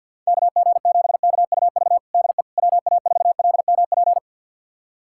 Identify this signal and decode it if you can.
Morse code, heard as a keyed tone, about 52 wpm, 700 Hz. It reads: KG7KUV DE WA4LMP
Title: USA Callsign DE Callsign - 6 Characters - 50wpm